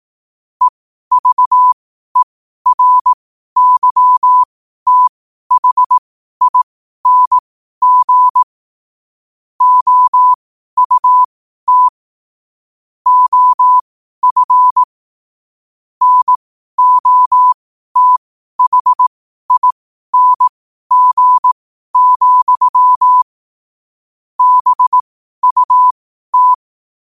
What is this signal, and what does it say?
Morse code, heard as a keyed tone, about 18 wpm, spaced as for 12 wpm, 1000 Hz. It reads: EVERYTHING OUT OF NOTHING, BUT